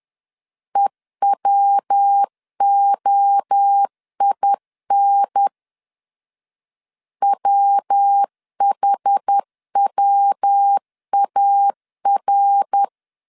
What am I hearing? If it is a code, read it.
Morse code, heard as a keyed tone, about 10 wpm, 800 Hz. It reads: EWOIN WHWAR
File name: CW / DET / 1 3 1500C S C 10wpm 3335